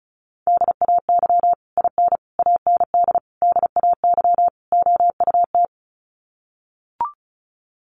Morse code, heard as a keyed tone, about 34 wpm, 700 Hz. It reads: DAY IN AND DAY OUT
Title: Sets of 5 Words Encore2 35wpm